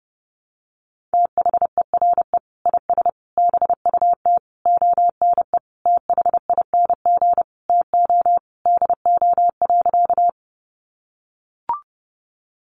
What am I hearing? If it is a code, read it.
Morse code, heard as a keyed tone, about 30 wpm, 700 Hz. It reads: THERE IS BUT ONE THING TO DO.
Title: Sentences from Top 100 Words 30wpm